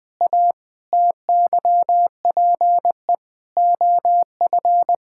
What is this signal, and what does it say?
Morse code, heard as a keyed tone, about 21 wpm, 700 Hz. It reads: A TYPE OF
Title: Sentences with 8-Word Phrases - 20wpm